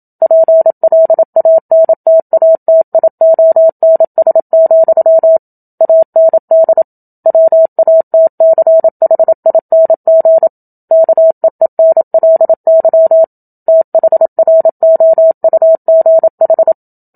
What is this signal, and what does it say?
Morse code, heard as a keyed tone, about 27 wpm, 650 Hz. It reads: PLANTATIONS, AND WATCHING KEENLY THROUGH
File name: Book Two / 01-Chapter 1 - 27 WPM